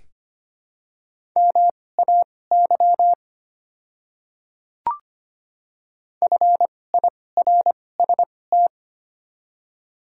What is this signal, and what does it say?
Morse code, heard as a keyed tone, about 25 wpm, spaced as for 18 wpm, 700 Hz. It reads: MAY FIRST
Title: 2x Character Spacing - Top 200 Words - 25wpm